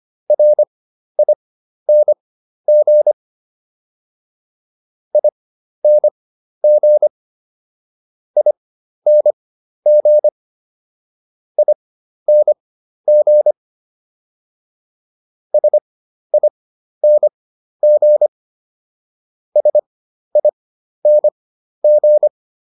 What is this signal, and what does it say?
Morse code, heard as a keyed tone, about 25 wpm, spaced as for 12 wpm, 600 Hz. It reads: RING ING ING ING SING SING